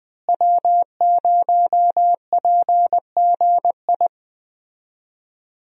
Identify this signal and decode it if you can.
Morse code, heard as a keyed tone, about 21 wpm, 700 Hz. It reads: W0PGI
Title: USA Call Signs - 5 Characters - Repeated 3x - 20wpm